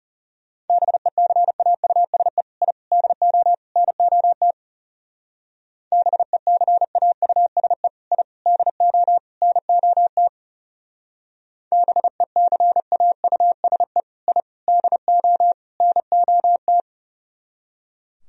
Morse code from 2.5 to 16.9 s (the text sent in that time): I DO NOT BECAUSE I DO NOT BECAUSE I DO NOT